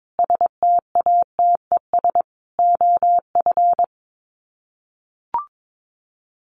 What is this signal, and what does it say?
Morse code, heard as a keyed tone, about 22 wpm, 700 Hz. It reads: STATES OF
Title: Sets of 5 Words Encore2 22wpm